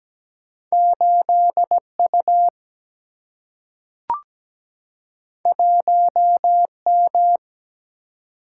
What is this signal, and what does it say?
Morse code, heard as a keyed tone, about 17 wpm, 700 Hz. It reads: 8 U 1 M
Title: Single-Letter Single-Number 17wpm